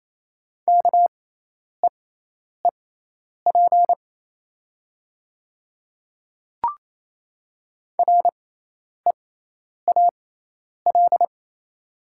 Morse code, heard as a keyed tone, about 28 wpm, spaced as for 10 wpm, 700 Hz. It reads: KEEP REAL
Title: 6x Character Spacing - Top 300 Words - 28wpm